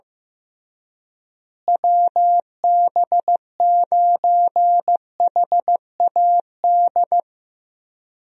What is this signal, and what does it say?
Morse code, heard as a keyed tone, about 15 wpm, 700 Hz. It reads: WB9HAD
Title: USA Call Signs - 6 Characters - Repeated 3x - 15wpm